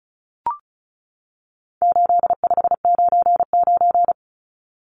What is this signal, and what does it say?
Morse code, heard as a keyed tone, about 34 wpm, 700 Hz. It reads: 8599
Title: Numbers - 4-Digit 35wpm